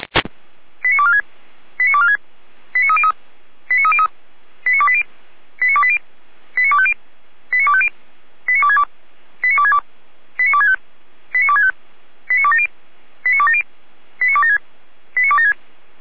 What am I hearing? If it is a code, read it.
Five-tone selective call (ZVEI), heard as quick runs of five tones, tones of about 70 ms. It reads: 89246 89246 89303 89303 89280 89280 89250 89250 89262 89262 89265 89265 89270 89270 89276 89276